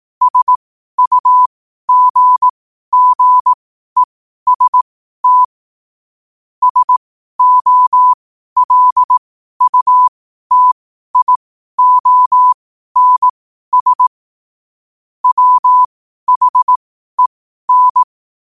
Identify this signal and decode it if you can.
Morse code, heard as a keyed tone, about 18 wpm, spaced as for 12 wpm, 1000 Hz. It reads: SUGGEST SOLUTIONS WHEN
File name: qotdfiles / 12wpm